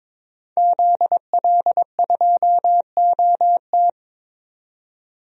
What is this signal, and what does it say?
Morse code, heard as a keyed tone, about 21 wpm, 700 Hz. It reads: ZL2OT
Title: DXCC-100 Call Signs - 5 Characters - 22wpm